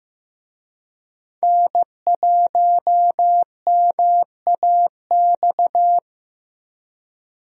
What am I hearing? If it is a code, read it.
Morse code, heard as a keyed tone, about 15 wpm, 700 Hz. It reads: N1MAX
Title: USA Call Signs - 5 Characters - Repeated 3x - 15wpm